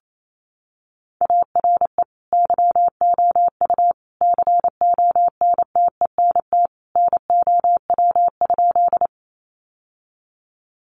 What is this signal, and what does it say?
Morse code, heard as a keyed tone, about 28 wpm, 700 Hz. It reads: ARE YOU CONTENT NOW?
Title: Sentences from Top 900 Words 28wpm